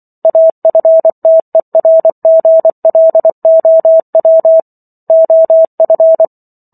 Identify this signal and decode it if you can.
Morse code, heard as a keyed tone, about 24 wpm, 650 Hz. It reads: AFTERGLOW OF